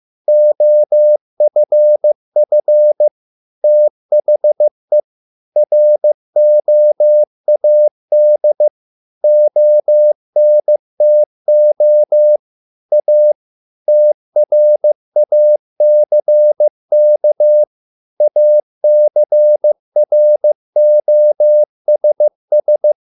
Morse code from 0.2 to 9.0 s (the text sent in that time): OFF THE ROAD